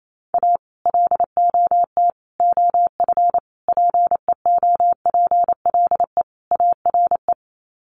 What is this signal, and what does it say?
Morse code, heard as a keyed tone, about 28 wpm, 700 Hz. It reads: A LOT OF PEOPLE ARE
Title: Sets of 5 Words 28wpm